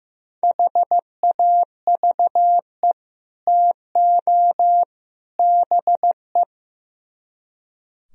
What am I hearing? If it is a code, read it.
Morse code, heard as a keyed tone, about 17 wpm, 700 Hz. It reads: HAVE TO BE
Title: Speed-Racing - Sets of 3 Words 15wpm